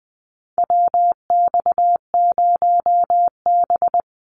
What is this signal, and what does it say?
Morse code, heard as a keyed tone, about 20 wpm, 700 Hz. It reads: WX0B
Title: Call Signs - 20wpm